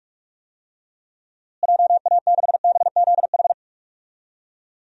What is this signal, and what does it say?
Morse code, heard as a keyed tone, about 46 wpm, 700 Hz. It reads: JA6BZH
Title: DXCC-100 Call Signs - 6 Characters - Repeated 2x - 45wpm